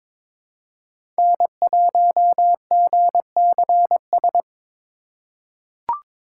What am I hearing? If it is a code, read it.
Morse code, heard as a keyed tone, about 22 wpm, 700 Hz. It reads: N1GCS
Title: USA Call Signs - 5 Characters - 2x Character Spacing - 22wpm